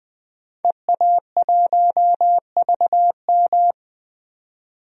Morse code, heard as a keyed tone, about 20 wpm, 700 Hz. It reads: EA1VM
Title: DXCC-100 Call Signs - 5 Characters - 20wpm